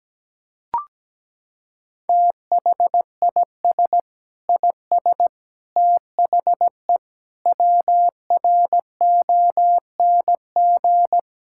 Morse code, heard as a keyed tone, about 17 wpm, 700 Hz. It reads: THIS IS THE WRONG
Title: Sentences from Top 1000 Words - Review Part 6 - 17wpm